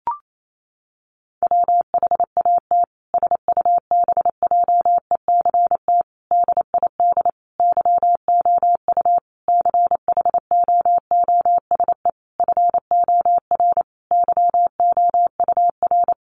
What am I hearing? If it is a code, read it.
Morse code, heard as a keyed tone, about 28 wpm, 700 Hz. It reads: WHAT SUBJECT DID YOU CHOOSE FOR YOUR
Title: Sentences from Top 1500 Words 28wpm